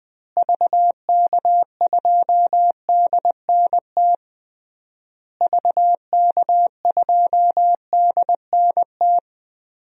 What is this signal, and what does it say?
Morse code, heard as a keyed tone, about 20 wpm, 700 Hz. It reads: VK2DNT VK2DNT
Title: DXCC-100 Call Signs - 6 Characters - Repeated 3x - 20wpm